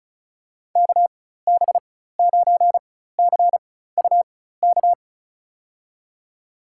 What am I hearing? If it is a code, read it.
Morse code, heard as a keyed tone, about 34 wpm, spaced as for 16 wpm, 700 Hz. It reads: KB9CUK
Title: USA Call Signs - 6 Characters - 4x Character Spacing - 35wpm